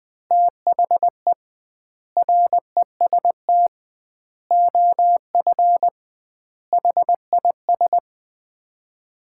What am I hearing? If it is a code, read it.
Morse code, heard as a keyed tone, about 20 wpm, 700 Hz. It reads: THE REST OF HIS
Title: 2x Word-Spacing - Sets of 5 Words - 20wpm